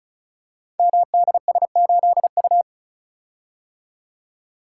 Morse code, heard as a keyed tone, about 36 wpm, 690 Hz. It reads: MDS8U
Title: Code Groups - 5 Letter:Numbers - 35wpm